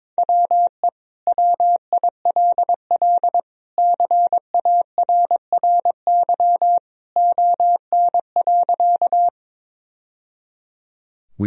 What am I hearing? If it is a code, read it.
Morse code, heard as a keyed tone, about 22 wpm, 700 Hz. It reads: WE WILL CARRY ON.
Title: Sentences from Top 1000 Words - Review Part 1 - 22wpm